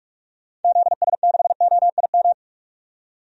Morse code, heard as a keyed tone, about 46 wpm, 700 Hz. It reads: ZS6OIM
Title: DXCC-100 Call Signs - 6 Characters - 45wpm